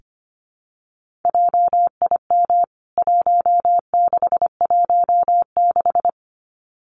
Morse code, heard as a keyed tone, about 25 wpm, 700 Hz. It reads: JIM 1616
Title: CWT Exchange 25wpm